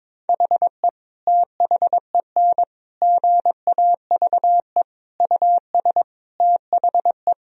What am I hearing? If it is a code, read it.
Morse code, heard as a keyed tone, about 22 wpm, 700 Hz. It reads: HE THEN GAVE US THE
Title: Sentences from Top 1000 Words - Review Part 4 - 22wpm